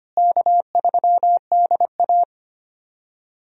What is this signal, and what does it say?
Morse code, heard as a keyed tone, about 25 wpm, 700 Hz. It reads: K3DA
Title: USA Call Signs - 4 Characters - Repeated 3x - 25wpm